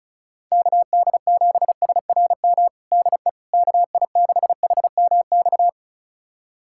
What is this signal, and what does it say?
Morse code, heard as a keyed tone, about 36 wpm, 700 Hz. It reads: KD7SRM DE KI6HMX
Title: USA Callsign DE Callsign 35wpm